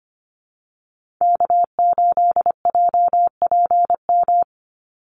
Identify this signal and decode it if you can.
Morse code, heard as a keyed tone, about 25 wpm, 700 Hz. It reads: K8JPM
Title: Call Signs Repeated 3x - 25wpm